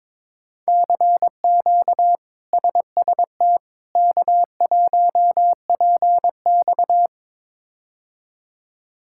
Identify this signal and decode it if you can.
Morse code, heard as a keyed tone, about 21 wpm, 700 Hz. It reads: CQ SST K1PX
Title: K1USN SST QSO Practice - 22wpm